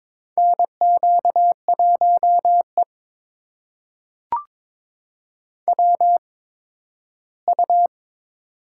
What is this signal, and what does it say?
Morse code, heard as a keyed tone, about 22 wpm, spaced as for 13 wpm, 700 Hz. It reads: NQ1E WU
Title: USA Call Signs - 4 Characters - 8x Character Spacing - 22wpm